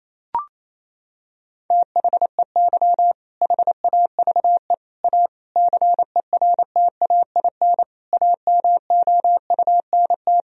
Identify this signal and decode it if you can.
Morse code, heard as a keyed tone, about 28 wpm, 700 Hz. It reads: THEY HAVE A CERTAIN AMOUNT
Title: Sentences from Top 1000 Words - Review Part 4 - 28wpm